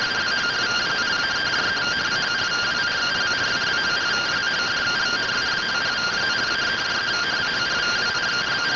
Two-tone teleprinter signal, about 170 Hz shift.